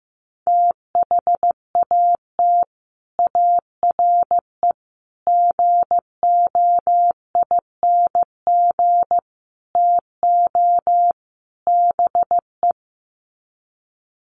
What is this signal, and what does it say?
Morse code, heard as a keyed tone, about 15 wpm, 700 Hz. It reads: THAT ARE GOING TO BE
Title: Sets of 5 Words Encore2 15wpm